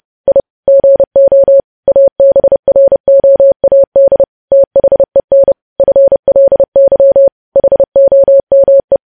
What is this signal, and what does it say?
Morse code, heard as a keyed tone, about 30 wpm, 550 Hz. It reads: I GO ABROAD THEN FLY HOME
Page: news headlines in morse code at 30 wpm